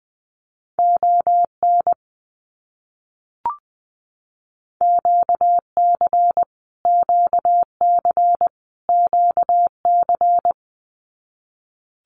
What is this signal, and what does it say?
Morse code, heard as a keyed tone, about 20 wpm, 700 Hz. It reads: ON QC QC QC
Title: Canadian Provinces and Territories - Repeated 3x - 20wpm